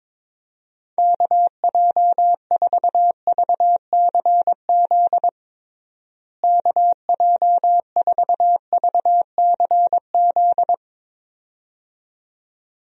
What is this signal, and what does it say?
Morse code, heard as a keyed tone, about 22 wpm, 700 Hz. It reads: KJ4VCZ KJ4VCZ
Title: USA Call Signs - 6 Characters - Repeated 2x - 22wpm